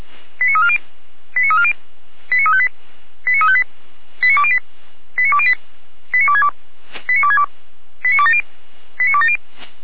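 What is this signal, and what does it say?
Five-tone selective call (ZVEI), heard as quick runs of five tones, tones of about 70 ms. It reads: 89340 89340 89358 89358 89208 89208 89262 89262 89270 89270
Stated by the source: ZVEI five-tone call 89110 received by SDR.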